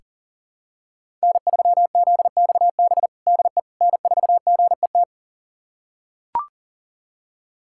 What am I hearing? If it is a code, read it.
Morse code, heard as a keyed tone, about 40 wpm, 700 Hz. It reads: N3ZXB DE N4GET
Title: USA Callsign DE Callsign - 5 Characters - 40wpm